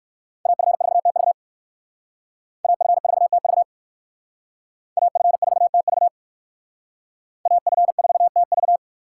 Morse code, heard as a keyed tone, about 56 wpm, 700 Hz. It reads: AF4TV AF4TV AF4TV AF4TV
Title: Speed Racing - USA 2x2 Call Signs 45wpm